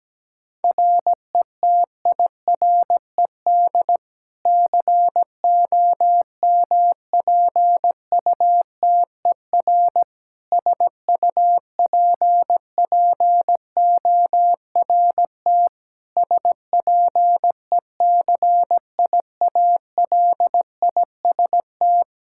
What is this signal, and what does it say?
Morse code, heard as a keyed tone, about 17 wpm, 700 Hz. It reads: RETIRED COMPUTER SUPPORT SPECIALIST
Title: Practice QSO - Exchange 3 - More Info - 17wpm